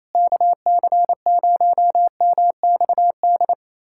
Morse code, heard as a keyed tone, about 28 wpm, 700 Hz. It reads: KC0MXD
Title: USA Callsign DE Callsign - 6 Characters - 28wpm